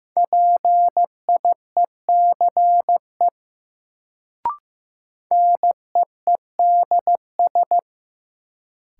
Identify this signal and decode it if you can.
Morse code, heard as a keyed tone, about 15 wpm, 700 Hz. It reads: PIECE NEEDS
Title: Sets of 3 Words - New Word Drill - 15wpm